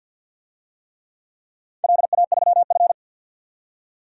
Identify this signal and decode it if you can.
Morse code, heard as a keyed tone, about 51 wpm, 700 Hz. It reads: LA3F
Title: DXCC-100 Call Signs - 4 Characters - 50wpm